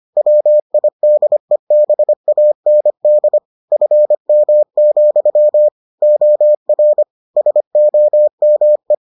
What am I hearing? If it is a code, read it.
Morse code, heard as a keyed tone, about 25 wpm, 600 Hz. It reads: WIDEBAND FM, OR SOME